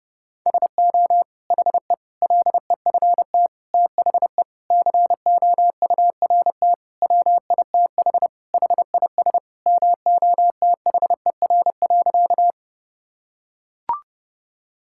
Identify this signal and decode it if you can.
Morse code, heard as a keyed tone, about 30 wpm, 700 Hz. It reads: SO HE LEFT THE COURT WITH HIS MOTHER.
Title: Sentences from Top 1600 Words 30wpm